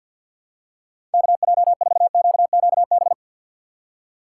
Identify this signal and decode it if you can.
Morse code, heard as a keyed tone, about 50 wpm, 700 Hz. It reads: KJ4QQB
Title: USA Call Signs - 6x Character Spacing - 50wpm